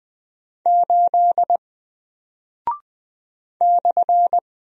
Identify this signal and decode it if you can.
Morse code, heard as a keyed tone, about 20 wpm, 700 Hz. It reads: 8 /
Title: Single Character (Letter:Number:Punctuation:Prosign) Alt 20wpm